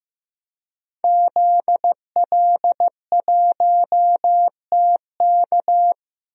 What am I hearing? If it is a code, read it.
Morse code, heard as a keyed tone, about 15 wpm, 700 Hz. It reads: ZL1TK
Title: DXCC-100 Call Signs - 5 Characters - 15wpm